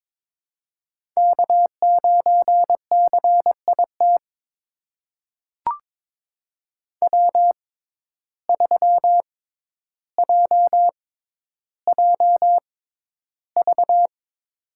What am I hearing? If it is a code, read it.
Morse code, heard as a keyed tone, about 22 wpm, spaced as for 11 wpm, 700 Hz. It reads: K9CIT W3JJV
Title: USA Call Signs - 5 Characters - 6x Character Spacing - 22wpm